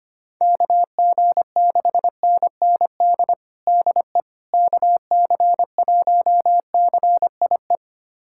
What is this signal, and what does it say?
Morse code, heard as a keyed tone, about 25 wpm, 700 Hz. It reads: KG6NND DE KC1CIE
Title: USA Callsign DE Callsign - 6 Characters - 25wpm